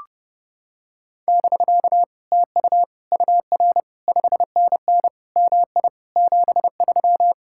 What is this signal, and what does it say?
Morse code, heard as a keyed tone, about 29 wpm, 700 Hz. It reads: <BK> TU UR 5NN MI 73